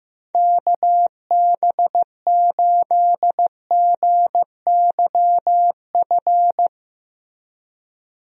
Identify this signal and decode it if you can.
Morse code, heard as a keyed tone, about 15 wpm, 700 Hz. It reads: KB8GYF
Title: USA Call Signs - 6 Characters - Repeated 3x - 15wpm